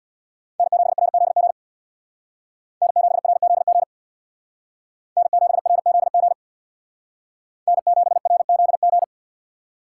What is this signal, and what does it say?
Morse code, heard as a keyed tone, about 63 wpm, 700 Hz. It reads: N7RZG N7RZG N7RZG N7RZG